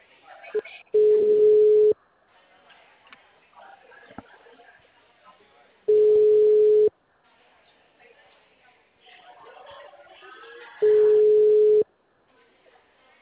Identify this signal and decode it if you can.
Ringback tone (Europe, 425 Hz), with speech.
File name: out- -2088- - - .390